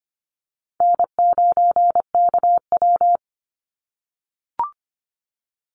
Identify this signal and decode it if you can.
Morse code, heard as a keyed tone, about 25 wpm, 700 Hz. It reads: N9KW
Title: NA SOTA Chaser Callsigns 25wpm